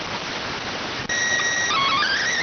SSTV (slow-scan television), PD 290.